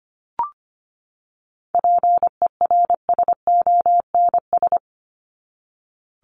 Morse code, heard as a keyed tone, about 25 wpm, 700 Hz. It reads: PERSONS
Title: ES:S Suffix - Top 100 Words with Base Words 25wpm